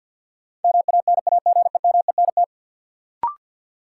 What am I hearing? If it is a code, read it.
Morse code, heard as a keyed tone, about 54 wpm, 700 Hz. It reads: MANAGEMENT
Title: Sentences from Top 1200 Words - New Word Drill Alt - 50wpm